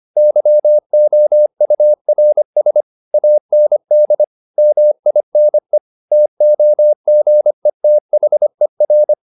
Morse code, heard as a keyed tone, about 25 wpm, 600 Hz. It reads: YOURS AND MINE TOGETHER
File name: the old man and the sea part 1 25wpm